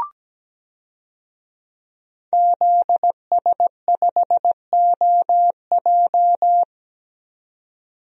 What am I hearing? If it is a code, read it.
Morse code, heard as a keyed tone, about 17 wpm, 700 Hz. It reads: ZS5OJ
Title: DXCC-100 Call Signs - 5 Characters - 17wpm